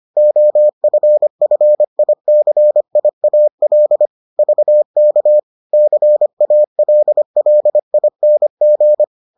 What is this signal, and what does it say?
Morse code, heard as a keyed tone, about 25 wpm, 600 Hz. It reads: OFFICIAL VK CALLING